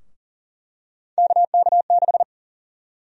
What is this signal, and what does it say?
Morse code, heard as a keyed tone, about 40 wpm, 700 Hz. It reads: KK6
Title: USA Call Sign Prefixes - Repeated 2x - 40wpm